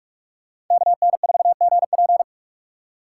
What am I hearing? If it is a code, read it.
Morse code, heard as a keyed tone, about 45 wpm, 700 Hz. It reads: KN4GP
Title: USA Call Signs - 3x Character Spacing - 45wpm